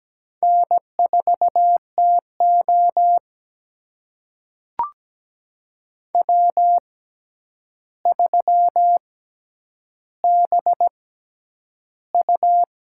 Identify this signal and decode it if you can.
Morse code, heard as a keyed tone, about 17 wpm, spaced as for 9 wpm, 700 Hz. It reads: N4TO W3BU
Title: USA Call Signs - 4 Characters - 6x Character Spacing - 17wpm